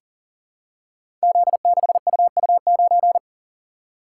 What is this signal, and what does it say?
Morse code, heard as a keyed tone, about 40 wpm, 700 Hz. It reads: Z6UU9
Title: Code Groups - 5 Letter:Numbers - 40wpm